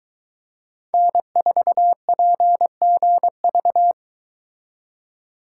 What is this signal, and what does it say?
Morse code, heard as a keyed tone, about 22 wpm, 700 Hz. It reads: N4PGV